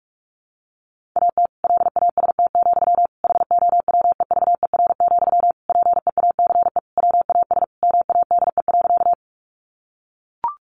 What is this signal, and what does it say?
Morse code, heard as a keyed tone, about 44 wpm, 700 Hz. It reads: AT LAST, HOWEVER, PEACE WAS MADE.
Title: Sentences from Top 1500 Words 45wpm